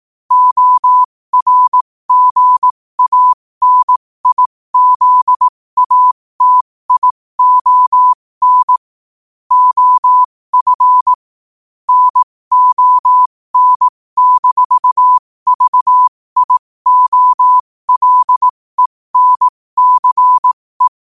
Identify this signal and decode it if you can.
Morse code, heard as a keyed tone, about 18 wpm, spaced as for 15 wpm, 1000 Hz. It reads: ORGANIZATION OF NON-VIOLENCE
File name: qotdfiles / 15wpm